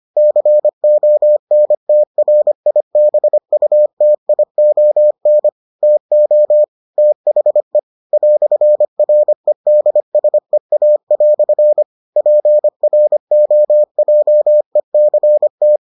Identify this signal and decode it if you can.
Morse code, heard as a keyed tone, about 25 wpm, 600 Hz. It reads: CONTRIBUTION TO THE "REDSEA" PROJECT